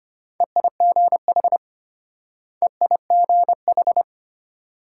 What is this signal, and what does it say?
Morse code, heard as a keyed tone, about 32 wpm, 700 Hz. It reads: EIGH EIGH